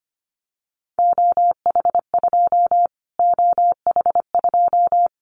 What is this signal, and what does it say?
Morse code, heard as a keyed tone, about 25 wpm, 700 Hz. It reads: OH2 OH2
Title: Callsign Prefixes for Top 25 DXCC - Repeated 3x - 25wpm